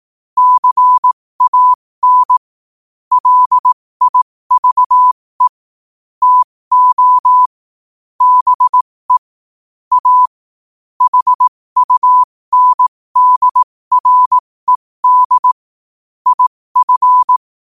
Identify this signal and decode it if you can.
Morse code, heard as a keyed tone, about 18 wpm, spaced as for 15 wpm, 1000 Hz. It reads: CAN LIVE TO BE A HUNDRED IF
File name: qotdfiles / 15wpm